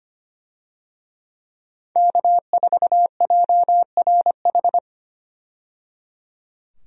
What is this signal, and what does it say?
Morse code, heard as a keyed tone, about 25 wpm, 700 Hz. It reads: K4JRH
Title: USA Call Signs - 5 Characters - 25wpm